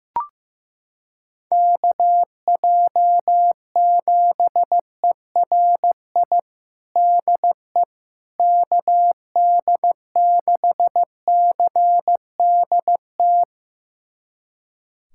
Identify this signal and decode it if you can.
Morse code, heard as a keyed tone, about 15 wpm, 700 Hz. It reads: KJ7ERI DE KD6CDT
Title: USA Callsign DE Callsign 15wpm